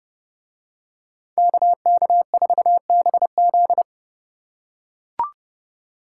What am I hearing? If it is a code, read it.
Morse code, heard as a keyed tone, about 31 wpm, 700 Hz. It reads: KK4BZ